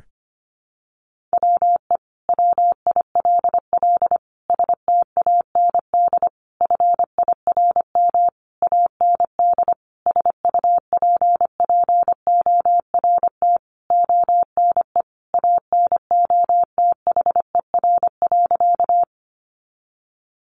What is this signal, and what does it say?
Morse code, heard as a keyed tone, about 25 wpm, 700 Hz. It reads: WE WILL STAND FIRM AND SUPPORT ONE ANOTHER.
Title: Sentences from Top 1400 Words 25wpm